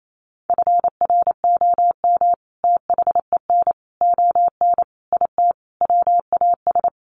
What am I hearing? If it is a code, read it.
Morse code, heard as a keyed tone, about 28 wpm, 700 Hz. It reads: FROM THEN ON IT WAS
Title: Sentences from Top 400 Words 28wpm